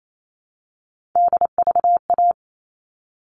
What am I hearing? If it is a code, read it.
Morse code, heard as a keyed tone, about 28 wpm, 700 Hz. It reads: DVA
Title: All Cut Numbers - 3-Digit 28wpm